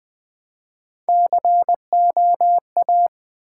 Morse code, heard as a keyed tone, about 20 wpm, 700 Hz. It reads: COA
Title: Build A Word - 1 Letter At A Time - 3 to 5-Letter Words - 20wpm